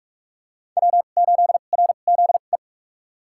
Morse code, heard as a keyed tone, about 46 wpm, spaced as for 32 wpm, 700 Hz. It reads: W8RZE
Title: USA Call Signs - 2x Character Spacing - 45wpm